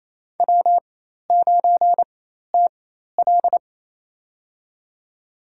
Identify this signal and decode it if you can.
Morse code, heard as a keyed tone, about 27 wpm, spaced as for 13 wpm, 700 Hz. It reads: W9TL
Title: USA Call Signs - 4 Characters - 4x Character Spacing - 28wpm